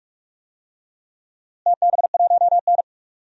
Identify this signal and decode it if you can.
Morse code, heard as a keyed tone, about 41 wpm, 700 Hz. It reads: TB1N
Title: DXCC-100 Call Signs - 4 Characters - Repeated 3x - 45wpm